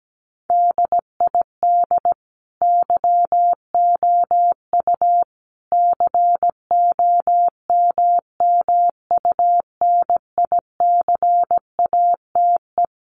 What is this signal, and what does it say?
Morse code, heard as a keyed tone, about 17 wpm, 700 Hz. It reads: DID YOU COMMUNICATE
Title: Sentences from Top 1300 Words 17wpm